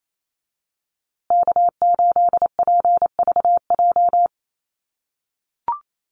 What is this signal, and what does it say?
Morse code, heard as a keyed tone, about 28 wpm, 700 Hz. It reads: K8PVJ
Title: USA Call Signs Repeated 3x 28wpm